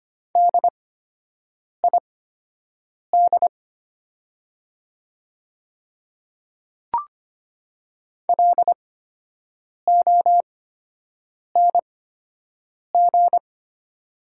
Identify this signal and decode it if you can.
Morse code, heard as a keyed tone, about 25 wpm, spaced as for 7 wpm, 700 Hz. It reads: DID LONG
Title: 8x Character Spacing - Top 100 Words - 25wpm